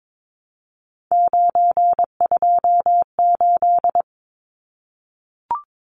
Morse code, heard as a keyed tone, about 22 wpm, 700 Hz. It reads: 928
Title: Numbers - 3-Digit 22wpm